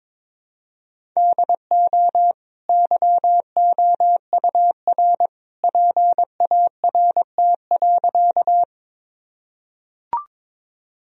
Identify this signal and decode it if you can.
Morse code, heard as a keyed tone, about 22 wpm, 700 Hz. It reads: DO YOUR PART.
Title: Sentences from Top 1000 Words - Review Part 1 - 22wpm